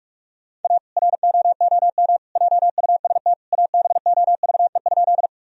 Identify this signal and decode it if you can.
Morse code, heard as a keyed tone, about 46 wpm, 700 Hz. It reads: A ROOM JUST ABOVE?
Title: Sentences from Top 1000 Words - Review Part 2 - 45wpm